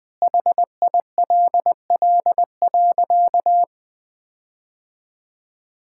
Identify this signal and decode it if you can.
Morse code, heard as a keyed tone, about 20 wpm, 700 Hz. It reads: HILL.
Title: Sentences with 6-Word Phrases - 20wpm